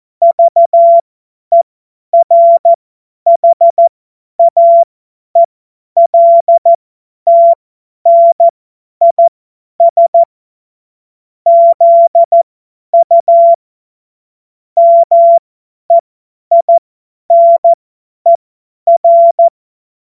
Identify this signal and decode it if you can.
Morse code, heard as a keyed tone, about 14 wpm, spaced as for 10 wpm, 680 Hz. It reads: VERHAELTNIS ZU MEINER
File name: Witz-08-Juli-2025-50-F70